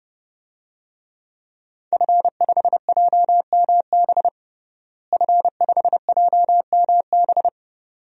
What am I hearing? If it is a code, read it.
Morse code, heard as a keyed tone, about 32 wpm, 700 Hz. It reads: F5JMB F5JMB
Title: DXCC-100 Call Signs - 5 Characters - Repeated 2x - 30wpm